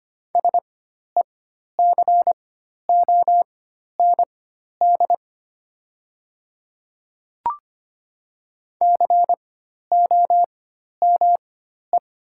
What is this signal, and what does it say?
Morse code, heard as a keyed tone, about 25 wpm, spaced as for 12 wpm, 700 Hz. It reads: SECOND COME
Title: 4x Character Spacing - Top 300 Words - 25wpm